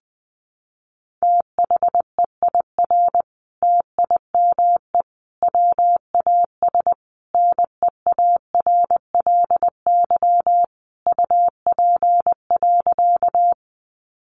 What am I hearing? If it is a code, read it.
Morse code, heard as a keyed tone, about 20 wpm, 700 Hz. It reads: THEIR TIME WAS NEARLY UP.
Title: Sentences from Top 1300 Words 20wpm